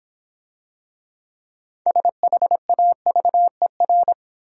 Morse code, heard as a keyed tone, about 26 wpm, 700 Hz. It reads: SHAVER